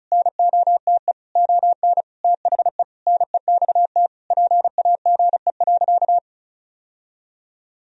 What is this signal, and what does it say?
Morse code, heard as a keyed tone, about 36 wpm, 700 Hz. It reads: NOTE ON THE NEXT PAGE.
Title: Sentences from Top 1000 Words - Review Part 2 - 35wpm